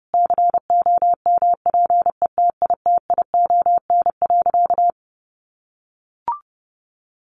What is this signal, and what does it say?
Morse code, heard as a keyed tone, about 30 wpm, 700 Hz. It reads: COMPETITION.
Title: Sentences from Top 1000 Words 30wpm